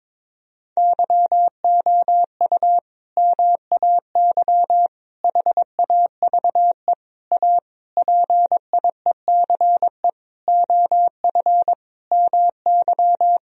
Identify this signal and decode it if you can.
Morse code, heard as a keyed tone, about 22 wpm, 700 Hz. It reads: YOU MAY HAVE A PIECE OF MY
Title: Sentences from Top 1000 Words - Review Part 2 - 22wpm